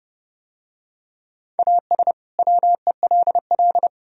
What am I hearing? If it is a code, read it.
Morse code, heard as a keyed tone, about 29 wpm, 700 Hz. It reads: AS WELL